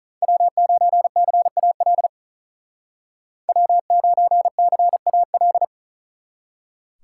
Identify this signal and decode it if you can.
Morse code, heard as a keyed tone, about 46 wpm, 700 Hz. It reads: W9CAL W9CAL